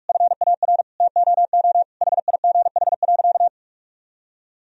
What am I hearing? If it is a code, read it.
Morse code, heard as a keyed tone, about 48 wpm, 700 Hz. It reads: FAR TOO HIGH.
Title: Sentences from Top 1000 Words - Review Part 1 - 45wpm